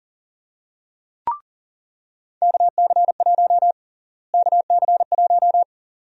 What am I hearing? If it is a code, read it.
Morse code, heard as a keyed tone, about 38 wpm, 700 Hz. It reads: KC1 KC1
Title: USA Call Sign Prefixes - Repeated 2x - 40wpm